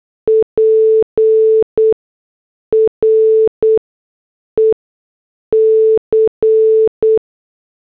Morse code, heard as a keyed tone, about 8 wpm, spaced as for 6 wpm, 440 Hz. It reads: PREC